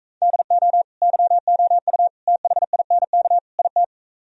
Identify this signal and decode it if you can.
Morse code, heard as a keyed tone, about 45 wpm, 700 Hz. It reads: DO YOU THINK IT